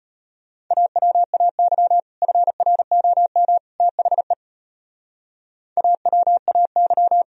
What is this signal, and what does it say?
Morse code, heard as a keyed tone, about 31 wpm, 700 Hz. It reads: AWAY FROM THE AWAY
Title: Speed-Racing - Sets of 3 Words 25wpm